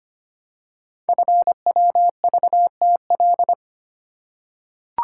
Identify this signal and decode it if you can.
Morse code, heard as a keyed tone, about 25 wpm, 700 Hz. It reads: FWVTL
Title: Code Groups - 5 Letter:Numbers - 25wpm